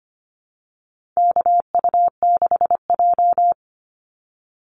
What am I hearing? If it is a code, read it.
Morse code, heard as a keyed tone, about 25 wpm, 700 Hz. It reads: KU6J
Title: NA SOTA Activator Callsigns 25wpm